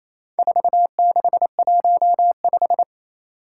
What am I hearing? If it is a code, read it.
Morse code, heard as a keyed tone, about 27 wpm, 700 Hz. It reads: 4615
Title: Four Challenging Letter-Number Combinations 28wpm